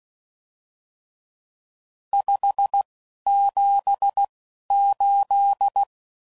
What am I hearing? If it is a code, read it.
Morse code, heard as a keyed tone, about 16 wpm, spaced as for 11 wpm, 800 Hz. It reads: 578
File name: CW Code Groups @10WPM / RSTs / Various RST No's